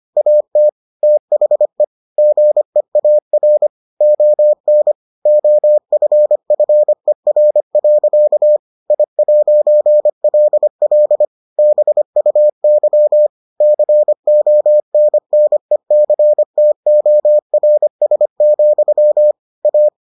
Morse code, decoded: AT THE GEAR ON OFFER. I'LL BUY CONNECTORS, A

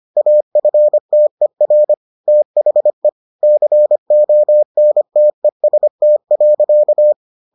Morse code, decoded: AFTER THE CONTEST.

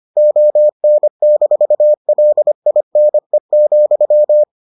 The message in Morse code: ON-LINE,

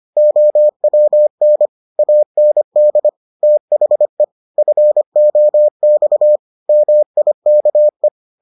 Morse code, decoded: OWN AND THE FOX MIKE

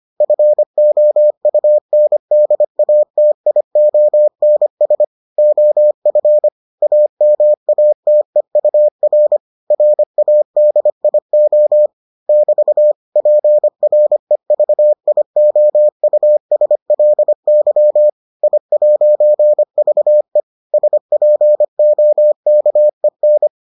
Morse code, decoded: FOUNDATIONS OF AMATEUR RADIO <BT> PREVIOUSLY I'VE SPOKEN

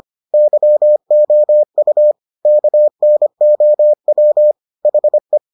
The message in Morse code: YOU KNOW HE

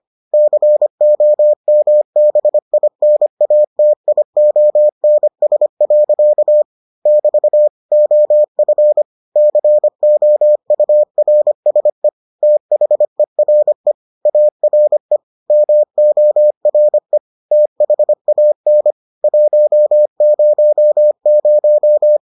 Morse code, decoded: COMBINATIONS. <BT> OF COURSE THERE ARE MORE THAN 100

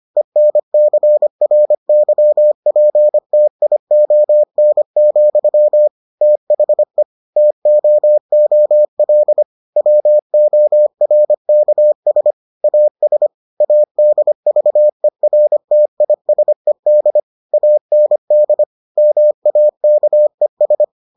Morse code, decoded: ENCRYPTION, THE TOOL WORKS AS ADVERTISED AND MAKES

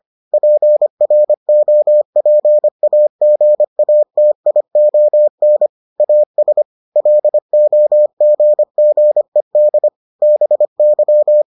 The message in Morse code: PROPAGATION AS LOGGED BY

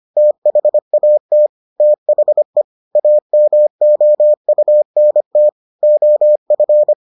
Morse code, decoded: THAT THE AMOUNT OF

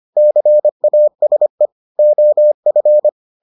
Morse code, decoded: CASE OF

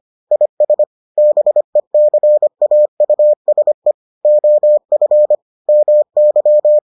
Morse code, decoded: IS BECAUSE OF MY

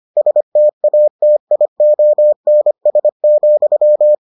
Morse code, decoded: STATIONS,

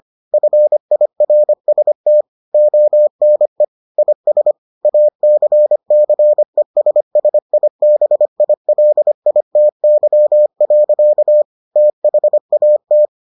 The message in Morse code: FIRST ONE IS ACCESSIBILITY. THAT

